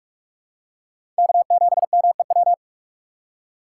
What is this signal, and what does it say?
Morse code, heard as a keyed tone, about 46 wpm, 700 Hz. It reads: K7MEW
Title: USA Call Signs - 2x Character Spacing - 45wpm